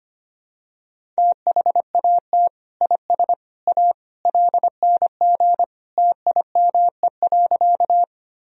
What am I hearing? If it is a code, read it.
Morse code, heard as a keyed tone, about 25 wpm, 700 Hz. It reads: THAT IS A LNG TIME.
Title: Sentences from Top 100 Words and Abbreviations 25wpm